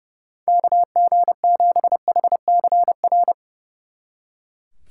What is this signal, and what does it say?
Morse code, heard as a keyed tone, about 36 wpm, 700 Hz. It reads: KG7HCR
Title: Speed Racing - USA 2x3 Call Signs 30wpm